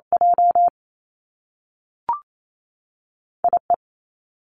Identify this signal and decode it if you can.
Morse code, heard as a keyed tone, about 28 wpm, 700 Hz. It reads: J I E
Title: Two Challenging Letter-Number Combinations 28wpm